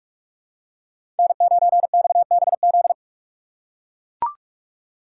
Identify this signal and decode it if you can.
Morse code, heard as a keyed tone, about 43 wpm, 700 Hz. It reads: N9XBZ
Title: USA Call Signs - 5 Characters - 6x Character Spacing - 45wpm